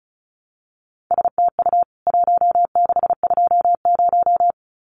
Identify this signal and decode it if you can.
Morse code, heard as a keyed tone, about 34 wpm, 700 Hz. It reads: STU 1620
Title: CWT Exchange 35wpm